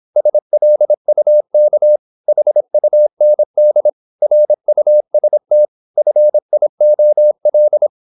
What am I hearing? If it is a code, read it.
Morse code, heard as a keyed tone, about 26 wpm, 600 Hz. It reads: SLUK HUND RUST FIOL
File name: Korte ord DK 26wpm